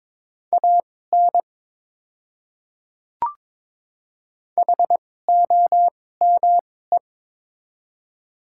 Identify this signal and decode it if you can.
Morse code, heard as a keyed tone, about 22 wpm, spaced as for 16 wpm, 700 Hz. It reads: AN HOME
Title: 2x Character Spacing - Top 300 Words - 22wpm